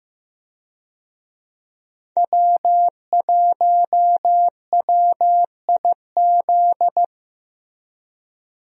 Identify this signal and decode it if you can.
Morse code, heard as a keyed tone, about 15 wpm, 700 Hz. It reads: W1WIZ
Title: USA Call Signs - 5 Characters - Repeated 3x - 15wpm